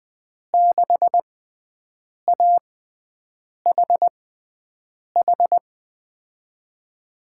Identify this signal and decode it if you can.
Morse code, heard as a keyed tone, about 21 wpm, spaced as for 7 wpm, 700 Hz. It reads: 6AHH
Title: USA Call Signs - 6 Characters - 6x Character Spacing - 20wpm